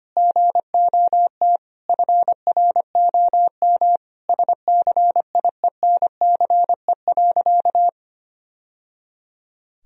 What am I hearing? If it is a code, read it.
Morse code, heard as a keyed tone, about 25 wpm, 700 Hz. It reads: GOT FROM SCIENCE.
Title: Sentences from Top 1000 Words - Review Part 2 - 25wpm